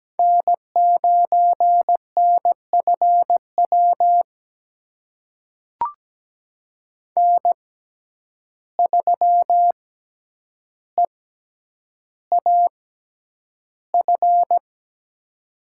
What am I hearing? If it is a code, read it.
Morse code, heard as a keyed tone, about 17 wpm, spaced as for 9 wpm, 700 Hz. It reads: N9NFW N3EAF
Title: USA Call Signs - 5 Characters - 6x Character Spacing - 17wpm